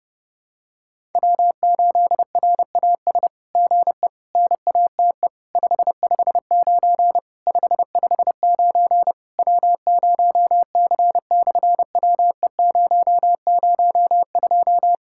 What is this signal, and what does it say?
Morse code, heard as a keyed tone, about 30 wpm, 700 Hz. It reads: W8RAS GE NATE 559 559 W0C/WE002
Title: SOTA QSO Practice 30wpm